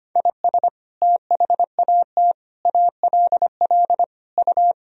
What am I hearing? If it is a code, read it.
Morse code, heard as a keyed tone, about 25 wpm, 700 Hz. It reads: IS THAT ALL U
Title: Sentences from Top 200 Words and Abbreviations 25wpm